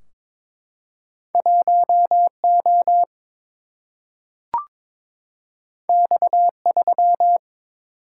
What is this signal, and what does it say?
Morse code, heard as a keyed tone, about 22 wpm, 700 Hz. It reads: 1 O X 3
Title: Single-Letter Single-Number 22wpm